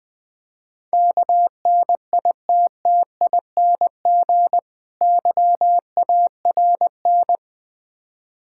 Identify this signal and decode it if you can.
Morse code, heard as a keyed tone, about 20 wpm, 700 Hz. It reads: KNITTING YARN
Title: Difficult Words - Speed-Racing with Hints - 20wpm